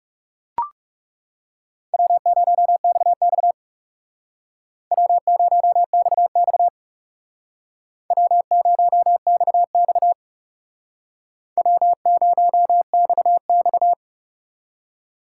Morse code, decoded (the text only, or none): W0XX W0XX W0XX W0XX